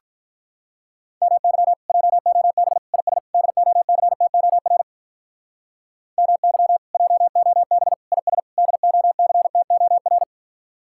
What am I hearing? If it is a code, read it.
Morse code, heard as a keyed tone, about 47 wpm, 700 Hz. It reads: MY JOB IS DOCTOR MY JOB IS DOCTOR